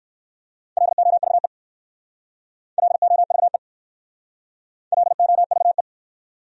Morse code, heard as a keyed tone, about 54 wpm, 690 Hz. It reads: LOVE LOVE LOVE
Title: Speed-Racing Top 500 Words 45wpm